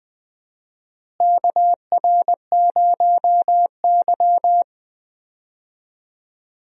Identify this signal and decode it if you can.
Morse code, heard as a keyed tone, about 20 wpm, 700 Hz. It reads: KR0Y
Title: USA Call Signs - 4 Characters - Repeated 2x - 20wpm